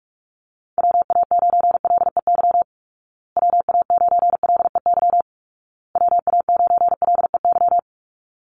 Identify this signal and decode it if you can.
Morse code, heard as a keyed tone, about 44 wpm, 700 Hz. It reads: WA9LEY WA9LEY WA9LEY
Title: Call Signs Repeated 2x - 45wpm